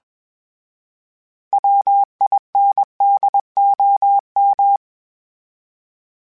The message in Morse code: WINDOM

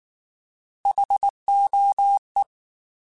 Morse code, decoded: HOE

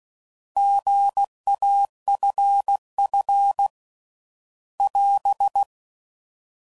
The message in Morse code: GAFF <AS>

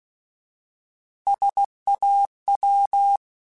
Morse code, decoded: SAW

